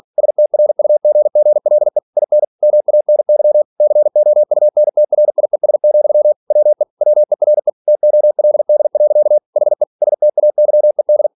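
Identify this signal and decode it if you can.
Morse code, heard as a keyed tone, about 46 wpm, 600 Hz. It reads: STRUGGLE IN MANY COUNTRIES, WE WERE TOLD. HE STAYED
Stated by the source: Practice text from an English novel sent at 47 WPM.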